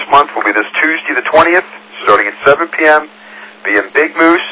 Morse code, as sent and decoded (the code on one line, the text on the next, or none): none